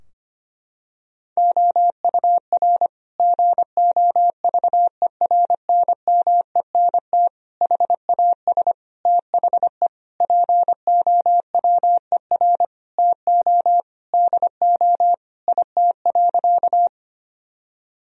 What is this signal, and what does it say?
Morse code, heard as a keyed tone, about 25 wpm, 700 Hz. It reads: OUR GOVERNMENT HAS THE POWER TO DO IT.
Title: Sentences from Top 1000 Words - Review Part 3 - 25wpm